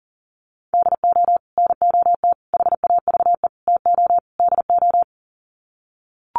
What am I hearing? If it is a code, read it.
Morse code, heard as a keyed tone, about 40 wpm, 700 Hz. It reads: DO NOT HAVE TO DO
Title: Sets of 5 Words 40wpm